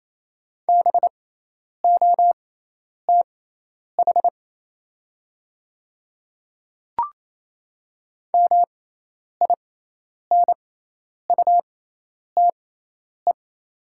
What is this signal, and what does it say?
Morse code, heard as a keyed tone, about 28 wpm, spaced as for 10 wpm, 700 Hz. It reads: BOTH MINUTE
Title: 6x Character Spacing - Top 500 Words - 28wpm